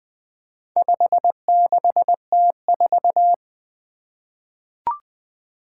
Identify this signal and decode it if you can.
Morse code, heard as a keyed tone, about 20 wpm, 700 Hz. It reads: 56T4
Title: Four Challenging Letter-Number Combinations 20wpm